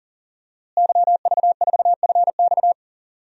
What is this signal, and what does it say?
Morse code, heard as a keyed tone, about 38 wpm, 700 Hz. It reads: YV4FX
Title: DXCC-100 Call Signs - 5 Characters - Repeated 3x - 40wpm